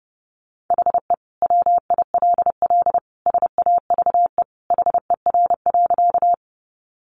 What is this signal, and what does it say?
Morse code, heard as a keyed tone, about 30 wpm, 700 Hz. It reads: HE WILL SAVE HER.
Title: Sentences from Top 1100 Words 30wpm